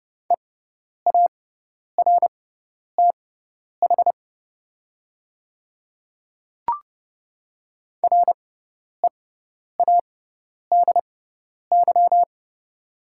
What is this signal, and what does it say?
Morse code, heard as a keyed tone, about 30 wpm, spaced as for 10 wpm, 700 Hz. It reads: EARTH READY
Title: 6x Character Spacing - Top 500 Words - 30wpm